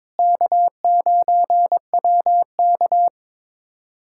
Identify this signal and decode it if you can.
Morse code, heard as a keyed tone, about 22 wpm, 700 Hz. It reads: K9WK
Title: USA Callsign DE Callsign - 4 Characters - 22wpm